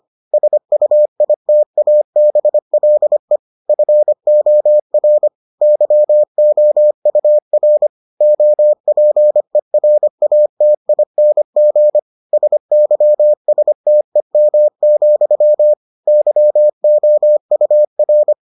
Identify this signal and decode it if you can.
Morse code, heard as a keyed tone, about 25 wpm, 600 Hz. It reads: SUITABLE FOR YOUR OPERATING SYSTEM, YOUR